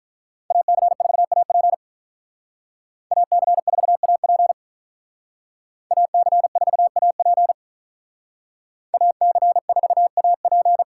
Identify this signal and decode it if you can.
Morse code, heard as a keyed tone, about 46 wpm, 700 Hz. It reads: AC4AP AC4AP AC4AP AC4AP